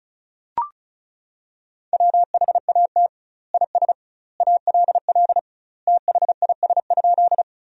Morse code, heard as a keyed tone, about 36 wpm, 700 Hz. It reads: WHAT IS ALL THIS?
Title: 2x Word-Spacing - Sentences from Top 100 Words - 35wpm